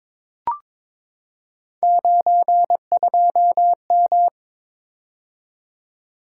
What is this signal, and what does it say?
Morse code, heard as a keyed tone, about 21 wpm, 700 Hz. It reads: 92M
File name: practice / 22wpm / Three Letter-Number Combinations - 22wpm